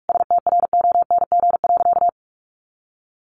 Morse code, heard as a keyed tone, about 44 wpm, 700 Hz. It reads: STRONG.
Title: Sentences from Top 1300 Words 45wpm